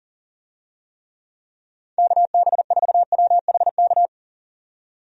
Morse code, heard as a keyed tone, about 40 wpm, 700 Hz. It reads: KB4WHK
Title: USA Call Signs - 6 Characters - 40wpm